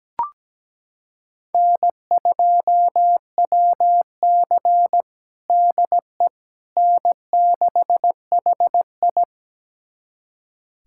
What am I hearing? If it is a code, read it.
Morse code, heard as a keyed tone, about 17 wpm, 700 Hz. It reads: N2WC DE N6HI
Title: USA Callsign DE Callsign - 4 Characters - 17wpm